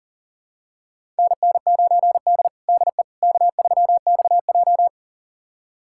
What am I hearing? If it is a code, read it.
Morse code, heard as a keyed tone, about 40 wpm, 700 Hz. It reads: NN9D DE K3XJ
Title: USA Callsign DE Callsign - 4 Characters - 40wpm